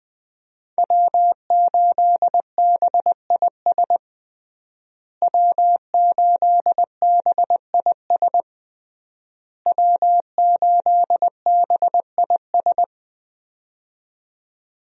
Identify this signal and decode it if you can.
Morse code, heard as a keyed tone, about 20 wpm, 700 Hz. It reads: W8BIS W8BIS W8BIS
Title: USA Call Signs - 5 Characters - Repeated 3x - 20wpm